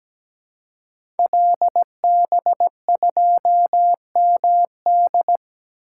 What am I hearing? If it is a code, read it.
Morse code, heard as a keyed tone, about 17 wpm, 700 Hz. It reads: LB2MD
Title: Code Groups - 5 Letter:Numbers - 17wpm